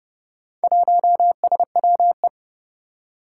Morse code, heard as a keyed tone, about 29 wpm, 700 Hz. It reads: 1SWE
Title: Four Challenging Letter-Number Combinations 30wpm